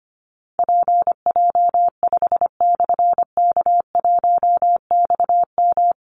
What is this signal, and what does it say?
Morse code, heard as a keyed tone, about 25 wpm, 700 Hz. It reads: PJ5/K1XM
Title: Call Signs Repeated 3x - 25wpm